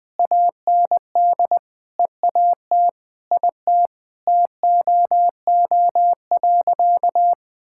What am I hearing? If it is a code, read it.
Morse code, heard as a keyed tone, about 21 wpm, 700 Hz. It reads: AND EAT IT TOO.
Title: Sentences with 9-Word Phrases - 20wpm